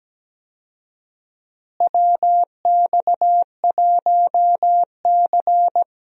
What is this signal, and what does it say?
Morse code, heard as a keyed tone, about 17 wpm, 700 Hz. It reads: WX1C
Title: USA Call Signs - 4 Characters - Repeated 2x - 17wpm